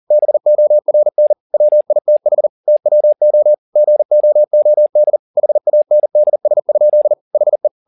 Morse code, heard as a keyed tone, about 40 wpm, 600 Hz. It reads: BORN WITH TWO GOOD HANDS? HE
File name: the old man and the sea 2 40wpm